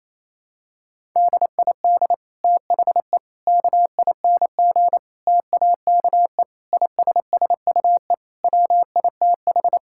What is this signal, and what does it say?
Morse code, heard as a keyed tone, about 28 wpm, 700 Hz. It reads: DID THE KING TAKE ISSUE WITH
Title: Sentences from Top 1000 Words - Review Part 6 - 28wpm